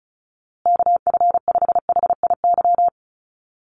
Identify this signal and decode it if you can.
Morse code, heard as a keyed tone, about 34 wpm, 700 Hz. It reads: KF5HIY
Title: USA Call Signs Repeated 3x 35wpm